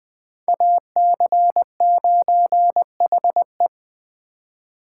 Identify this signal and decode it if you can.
Morse code, heard as a keyed tone, about 20 wpm, 700 Hz. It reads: AC9HE
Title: Speed Racing - USA 2x2 Call Signs 17wpm